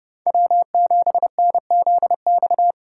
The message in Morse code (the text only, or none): W7NZX